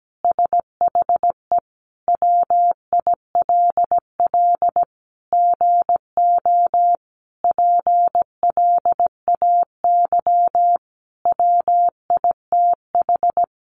Morse code, heard as a keyed tone, about 17 wpm, 700 Hz. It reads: SHE WILL GO PLAY WITH
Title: Sentences from Top 1500 Words 17wpm